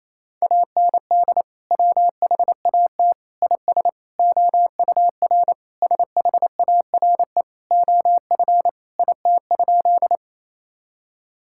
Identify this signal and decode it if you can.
Morse code, heard as a keyed tone, about 28 wpm, 700 Hz. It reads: AND WHAT IS OUR SHARE OF IT?
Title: Sentences from Top 1000 Words - Review Part 6 - 28wpm